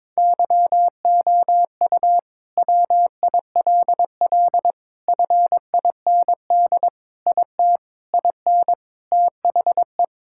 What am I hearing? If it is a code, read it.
Morse code, heard as a keyed tone, about 22 wpm, 700 Hz. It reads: YOU WILL FIND IT IN THE
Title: Sentences from Top 1000 Words - Review Part 5 - 22wpm